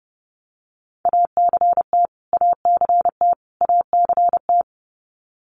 Morse code, heard as a keyed tone, about 30 wpm, 700 Hz. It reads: ACT ACT ACT
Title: Top 300 Words Repeated 30wpm